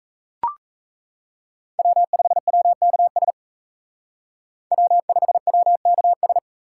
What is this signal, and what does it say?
Morse code, heard as a keyed tone, about 36 wpm, 700 Hz. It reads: W5WKS W5WKS